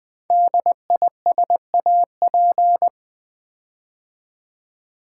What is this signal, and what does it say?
Morse code, heard as a keyed tone, about 20 wpm, 700 Hz. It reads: DISAP
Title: Build A Word - 5 Letters At A Time - 12-Letter Words - 20wpm